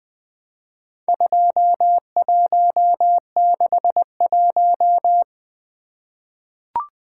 Morse code, heard as a keyed tone, about 20 wpm, 700 Hz. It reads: 2161
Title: Four Letter-Number Combinations - 20wpm